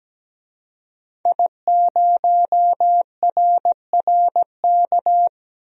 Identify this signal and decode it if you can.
Morse code, heard as a keyed tone, about 17 wpm, 700 Hz. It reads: I0RRK
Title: DXCC-100 Call Signs - 5 Characters - 17wpm